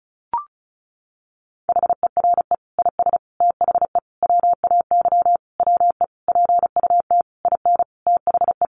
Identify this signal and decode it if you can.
Morse code, heard as a keyed tone, about 34 wpm, 700 Hz. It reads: HERE IS THE WAY WE PUT IN THE
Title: Sentences from Top 200 Words 35wpm